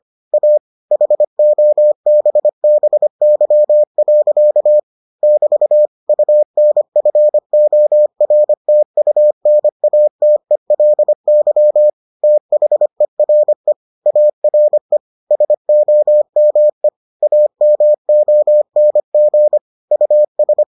Morse code, heard as a keyed tone, about 25 wpm, 600 Hz. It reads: A HOBBY. <BT> UNFORTUNATELY THERE ARE SOME AMONG US